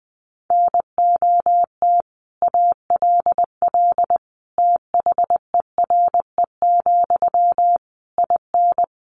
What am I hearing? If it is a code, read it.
Morse code, heard as a keyed tone, about 20 wpm, 700 Hz. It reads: NOT ALL THERE, IN
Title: Sentences from Top 600 Words 20wpm